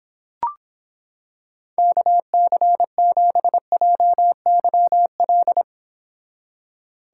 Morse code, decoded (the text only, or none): KC7JYL